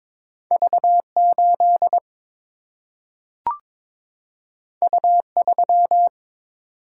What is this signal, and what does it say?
Morse code, heard as a keyed tone, about 22 wpm, 700 Hz. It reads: V 8 U 3
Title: Single-Letter Single-Number 22wpm